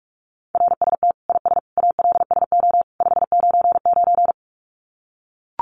Morse code, heard as a keyed tone, about 44 wpm, 700 Hz. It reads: RST IS ALSO 599
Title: Common QSO Phrases 45wpm